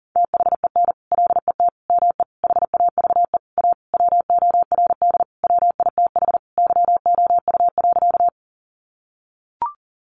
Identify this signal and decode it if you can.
Morse code, heard as a keyed tone, about 40 wpm, 700 Hz. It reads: THEN LET ME HAVE A WORD WITH YOU.
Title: Sentences from Top 300 Words 40wpm